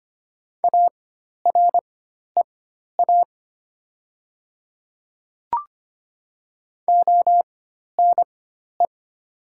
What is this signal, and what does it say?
Morse code, heard as a keyed tone, about 25 wpm, spaced as for 12 wpm, 700 Hz. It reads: AREA ONE
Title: 4x Character Spacing - Top 500 Words - 25wpm